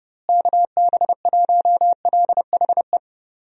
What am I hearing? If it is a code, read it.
Morse code, heard as a keyed tone, about 30 wpm, 700 Hz. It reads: KB1LHE
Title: USA Call Signs - 6 Characters - Repeated 2x - 30wpm